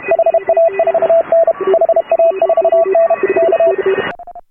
Morse code, heard as a keyed tone, about 32 wpm, 660 Hz, with noise. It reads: HA4N HA4N K